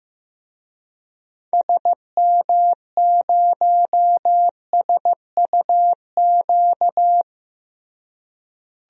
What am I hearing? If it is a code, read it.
Morse code, heard as a keyed tone, about 15 wpm, 700 Hz. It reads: SM0SUQ
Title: DXCC-100 Call Signs - 6 Characters - Repeated 3x - 15wpm